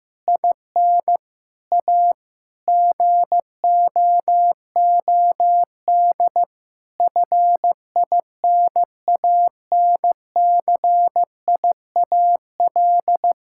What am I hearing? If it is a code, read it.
Morse code, heard as a keyed tone, about 16 wpm, 700 Hz. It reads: IN A GOOD FINANCIAL
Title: Sentences from Top 1000 Words - Review Part 5 - 15wpm